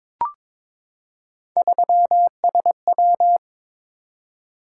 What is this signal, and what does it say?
Morse code, heard as a keyed tone, about 22 wpm, 700 Hz. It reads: 3SW
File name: practice / 22wpm / Three Letter-Number Combinations - 22wpm